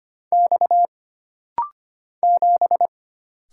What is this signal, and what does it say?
Morse code, heard as a keyed tone, about 25 wpm, 700 Hz. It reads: X 7
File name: Single Character (Letter:Number:Punctuation:Prosign) - Rapid-Fire - 25wpm